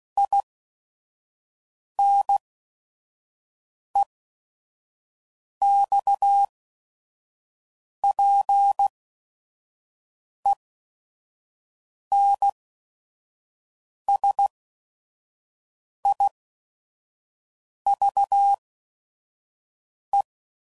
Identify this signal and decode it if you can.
Morse code, heard as a keyed tone, about 16 wpm, spaced as for 5 wpm, 800 Hz. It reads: INEXPENSIVE
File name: Sayings / Quotes / A Smile